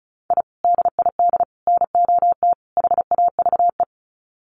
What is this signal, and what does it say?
Morse code, heard as a keyed tone, about 34 wpm, 700 Hz. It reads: I DID NOT HAVE
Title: Sets of 4 Words 35wpm